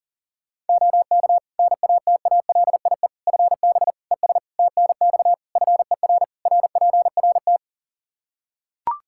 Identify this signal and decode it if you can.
Morse code, heard as a keyed tone, about 37 wpm, 700 Hz. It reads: OK NATALIE FB ES TNX FER RPRT
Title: Practice QSO - Exchange 2 - Info - 40wpm